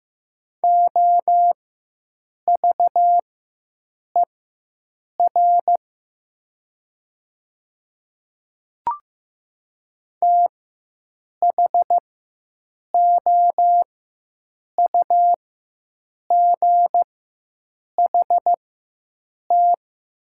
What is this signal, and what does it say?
Morse code, heard as a keyed tone, about 15 wpm, spaced as for 7 wpm, 700 Hz. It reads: OVER THOUGHT
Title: 4x Character Spacing - Top 500 Words - 15wpm